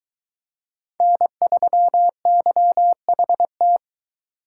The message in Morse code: N3YHT

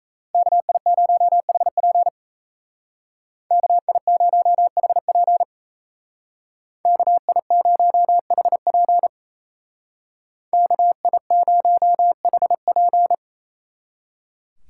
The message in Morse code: KI0HP KI0HP KI0HP KI0HP

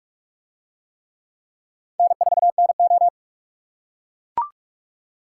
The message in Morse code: N4NO

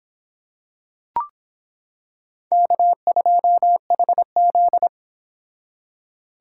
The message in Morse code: K2HZ

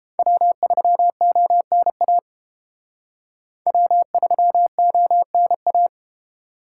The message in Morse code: W3ONA W3ONA

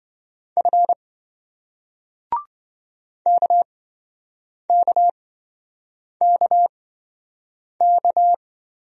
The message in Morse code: F KKKK

